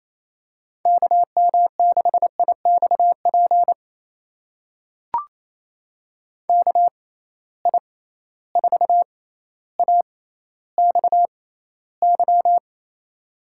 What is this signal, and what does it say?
Morse code, heard as a keyed tone, about 27 wpm, spaced as for 14 wpm, 700 Hz. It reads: KM6IXP KI4AXY
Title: USA Call Signs - 6 Characters - 6x Character Spacing - 28wpm